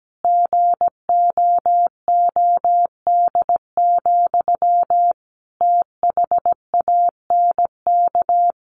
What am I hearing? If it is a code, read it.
Morse code, heard as a keyed tone, about 17 wpm, 700 Hz. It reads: GOOD, THANK
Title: Sentences from Top 1100 Words 17wpm